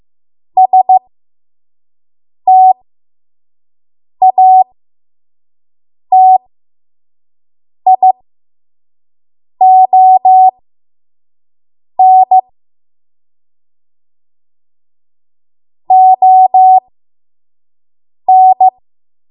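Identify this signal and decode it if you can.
Morse code, heard as a keyed tone, about 15 wpm, spaced as for 5 wpm, 750 Hz. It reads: STATION ON
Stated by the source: 5 WPM Morse code practice.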